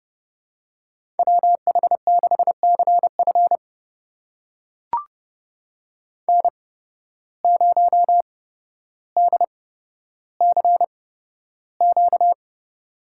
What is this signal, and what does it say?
Morse code, heard as a keyed tone, about 30 wpm, spaced as for 13 wpm, 700 Hz. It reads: WH6CF N0DCQ
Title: USA Call Signs - 5 Characters - 8x Character Spacing - 30wpm